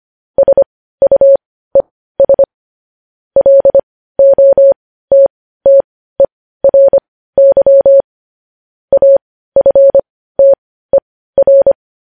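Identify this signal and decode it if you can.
Morse code, heard as a keyed tone, about 25 wpm, spaced as for 15 wpm, 550 Hz. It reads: SUES LOTTERY AFTER
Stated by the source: Morse code news headlines sent at 15.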